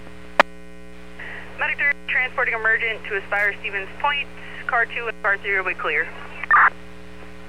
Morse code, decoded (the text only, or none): none